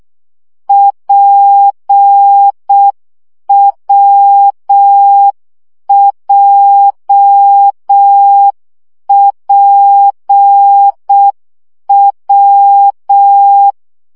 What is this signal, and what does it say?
Morse code, heard as a keyed tone, about 6 wpm, 800 Hz. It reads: PWJPW
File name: CW gyakorlas w p j 5 baud